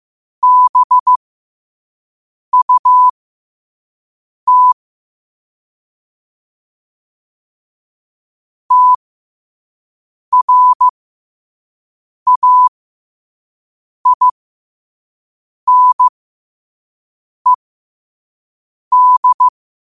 Morse code, heard as a keyed tone, about 15 wpm, spaced as for 5 wpm, 1000 Hz. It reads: BUT TRAINED
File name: qotdfiles / 5wpm